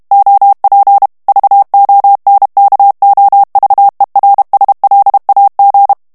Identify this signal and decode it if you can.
Morse code, heard as a keyed tone, about 32 wpm, 800 Hz. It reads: OP VONKOVERSLAG
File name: morse mp3 / HOLLANDAIS texte clair 2 vitesse 30 mots minute